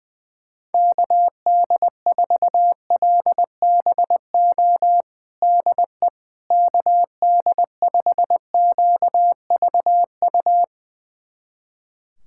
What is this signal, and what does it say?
Morse code, heard as a keyed tone, about 20 wpm, 700 Hz. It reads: KD4LBO DE KD5QVU
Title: USA Callsign DE Callsign - 6 Characters - 20wpm